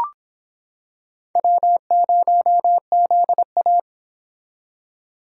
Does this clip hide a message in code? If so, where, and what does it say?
Yes: Morse code from 1.4 to 3.8 s, heard as a keyed tone, about 26 wpm, 700 Hz. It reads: W0ZA